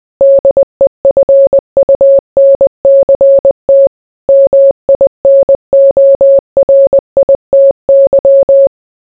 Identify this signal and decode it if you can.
Morse code, heard as a keyed tone, about 20 wpm, 550 Hz. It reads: DEFUNCT MINORITY